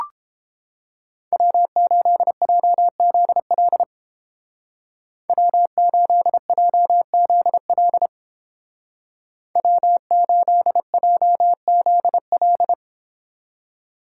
Morse code, heard as a keyed tone, about 29 wpm, 700 Hz. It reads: W8JZL W8JZL W8JZL